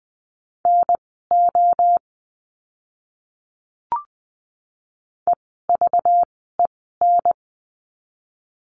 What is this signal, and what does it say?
Morse code, heard as a keyed tone, about 20 wpm, spaced as for 15 wpm, 700 Hz. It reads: NO EVEN
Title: 230 Lesson 37f - Letter x - Words 20wpm